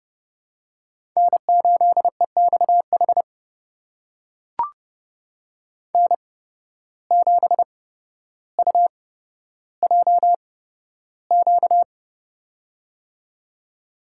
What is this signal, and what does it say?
Morse code, heard as a keyed tone, about 30 wpm, spaced as for 13 wpm, 700 Hz. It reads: N8EXH N7UJQ
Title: USA Call Signs - 5 Characters - 8x Character Spacing - 30wpm